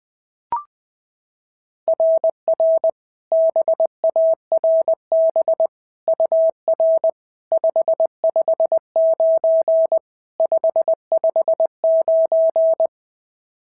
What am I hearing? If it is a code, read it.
Morse code, heard as a keyed tone, about 20 wpm, 690 Hz. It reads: RR BARB UR 559 559
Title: SOTA QSO Practice 20wpm